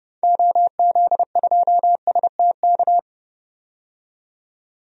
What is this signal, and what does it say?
Morse code, heard as a keyed tone, about 30 wpm, 700 Hz. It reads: OZ2STK
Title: DXCC-100 Call Signs - 6 Characters - 30wpm